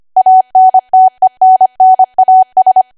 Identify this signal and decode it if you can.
Morse code, heard as a keyed tone, about 25 wpm, 750 Hz. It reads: ANTENNAS
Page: 25 WPM Code Archive